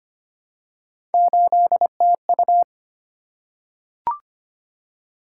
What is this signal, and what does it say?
Morse code, heard as a keyed tone, about 25 wpm, 700 Hz. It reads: 8TU
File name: Three Letter-Number Combinations - 25wpm